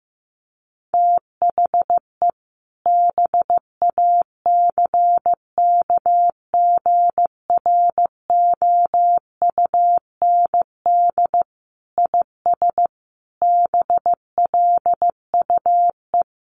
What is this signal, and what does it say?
Morse code, heard as a keyed tone, about 15 wpm, 700 Hz. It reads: THE BACKGROUND IS BLUE
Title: Sentences from Top 1300 Words 15wpm